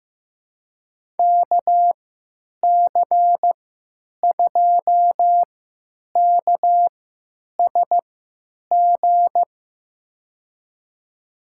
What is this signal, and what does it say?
Morse code, heard as a keyed tone, about 15 wpm, spaced as for 9 wpm, 700 Hz. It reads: KC2KSG
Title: USA Call Signs - 6 Characters - 3x Character Spacing - 15wpm